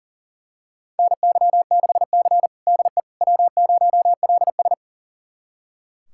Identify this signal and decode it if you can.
Morse code, heard as a keyed tone, about 40 wpm, 700 Hz. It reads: NY6C DE W0LS
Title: USA Callsign DE Callsign - 4 Characters - 40wpm